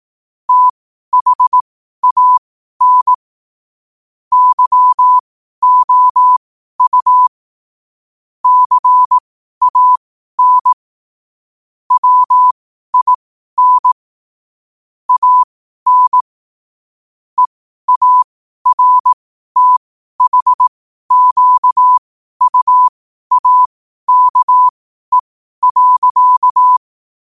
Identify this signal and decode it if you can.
Morse code, heard as a keyed tone, about 18 wpm, spaced as for 12 wpm, 1000 Hz. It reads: THAN YOU CAN WIN AN EARTHQUAKE.
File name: qotdfiles / 12wpm